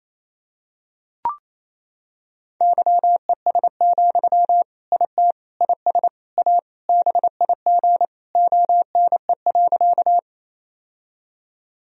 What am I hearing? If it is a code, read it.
Morse code, heard as a keyed tone, about 28 wpm, 700 Hz. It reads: YES, IT IS A BIG ONE.
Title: Sentences from Top 1000 Words - Review Part 1 - 28wpm